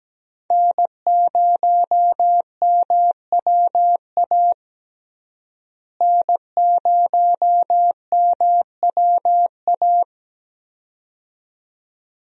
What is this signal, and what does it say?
Morse code, heard as a keyed tone, about 17 wpm, 700 Hz. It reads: N0MWA N0MWA
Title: USA Call Signs - 5 Characters - Repeated 2x - 17wpm